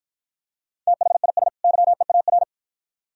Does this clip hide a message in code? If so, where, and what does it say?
Yes: Morse code from 0.9 to 2.4 s, heard as a keyed tone, about 52 wpm, 700 Hz. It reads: THIS YEAR